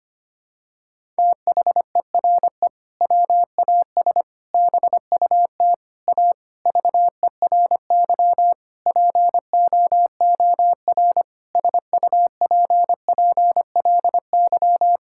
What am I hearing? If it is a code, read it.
Morse code, heard as a keyed tone, about 25 wpm, 700 Hz. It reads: THERE WAS BUT A VERY POOR SUPPLY
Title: Sentences from Top 1000 Words - Review Part 5 - 25wpm